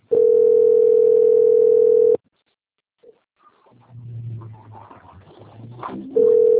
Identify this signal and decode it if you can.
Ringback tone (North America), 440 + 480 Hz.
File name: out- - - - - .7978